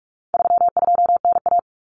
Morse code, heard as a keyed tone, about 44 wpm, 700 Hz. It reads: 32NA
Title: Common Cut Numbers - 4-Digit 45wpm